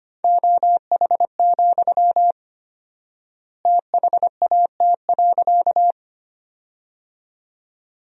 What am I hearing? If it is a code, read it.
Morse code, heard as a keyed tone, about 25 wpm, 700 Hz. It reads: OH, THAT.
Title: 4x Word-Spacing - Sentences from Top 500 Words - 25wpm